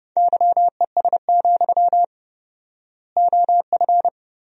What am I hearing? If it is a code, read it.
Morse code, heard as a keyed tone, about 30 wpm, 700 Hz. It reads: YES, OF
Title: 4x Word-Spacing - Sentences from Top 500 Words - 30wpm